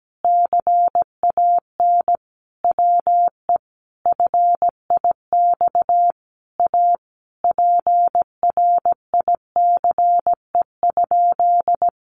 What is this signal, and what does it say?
Morse code, heard as a keyed tone, about 17 wpm, 700 Hz. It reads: CAN WE FIX A PRICE?
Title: Sentences from Top 1400 Words 17wpm